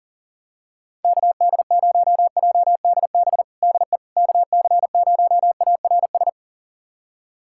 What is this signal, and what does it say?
Morse code, heard as a keyed tone, about 39 wpm, 700 Hz. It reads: KD0JDB DE KC0ARS
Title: USA Callsign DE Callsign - 6 Characters - 40wpm